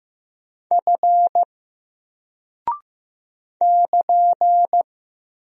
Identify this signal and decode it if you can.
Morse code, heard as a keyed tone, about 15 wpm, 700 Hz. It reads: F <KN>
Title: Single Character (Letter:Number:Punctuation:Prosign) Alt 15wpm